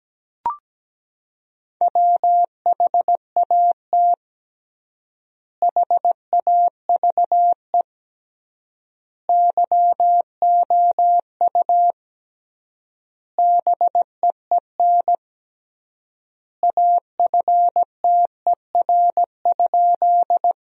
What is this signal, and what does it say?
Morse code, heard as a keyed tone, about 17 wpm, 700 Hz. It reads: WHAT HAVE YOU BEEN AFTER?
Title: 3x Word-Spacing - Sentences from Top 200 Words - 17wpm